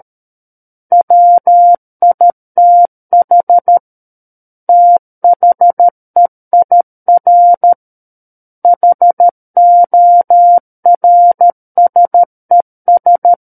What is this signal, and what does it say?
Morse code, heard as a keyed tone, about 13 wpm, 700 Hz. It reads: WITH THEIR HORSES
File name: Book One / 04-Chapter 4 - 13 WPM